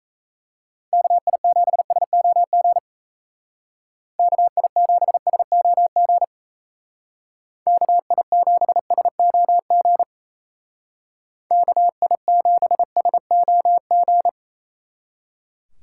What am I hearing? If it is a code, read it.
Morse code, heard as a keyed tone, about 33 wpm, 700 Hz. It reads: KI7SOG KI7SOG KI7SOG KI7SOG
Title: Speed Racing - USA 2x3 Call Signs 28wpm